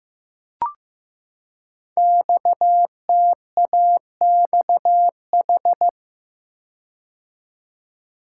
Morse code, heard as a keyed tone, about 15 wpm, 700 Hz. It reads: XTAXH
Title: Code Groups - 5 Letters - 15wpm